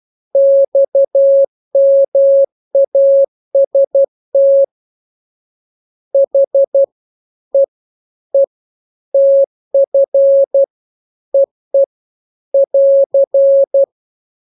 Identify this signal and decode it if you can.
Morse code, heard as a keyed tone, about 12 wpm, 550 Hz. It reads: XMAST H E E TF EE <AR>
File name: jokes / joke 16 12 0000